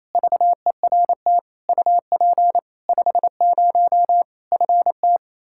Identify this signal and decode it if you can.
Morse code, heard as a keyed tone, about 28 wpm, 700 Hz. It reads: VERT UP 50 FT
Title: Practice QSO - Exchange 2 - Info - 28wpm